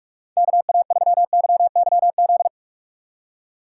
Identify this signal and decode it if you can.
Morse code, heard as a keyed tone, about 46 wpm, 700 Hz. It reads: KA3YYZ
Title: USA Call Signs - 3x Character Spacing - 45wpm